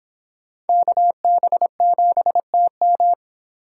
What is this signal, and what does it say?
Morse code, heard as a keyed tone, about 30 wpm, 700 Hz. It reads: KB7TM